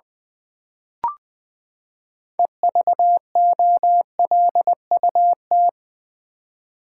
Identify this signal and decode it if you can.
Morse code, heard as a keyed tone, about 20 wpm, 700 Hz. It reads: EVOLUT
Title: Build A Word - 6 Letters At A Time - 12-Letter Words - 20wpm